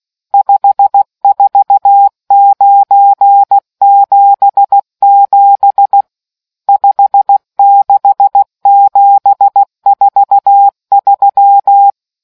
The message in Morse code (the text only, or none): 54977 56743